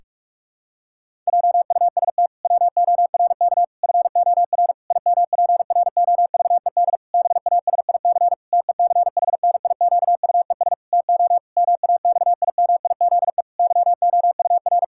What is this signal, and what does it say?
Morse code, heard as a keyed tone, about 43 wpm, 700 Hz. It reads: JUST WORK FOR IMPROVED BASIC TECHNIQUES TO MAXIMIZE YOUR
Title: Pangram Challenge v2 - 45wpm